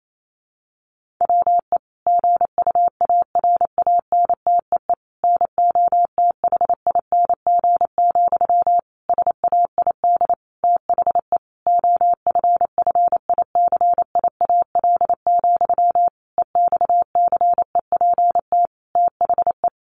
Morse code, heard as a keyed tone, about 28 wpm, 700 Hz. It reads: WE GUARANTEE NOTHING, SAID THE OFFICIAL, EXCEPT THE
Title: A Sound Of Thunder 28wpm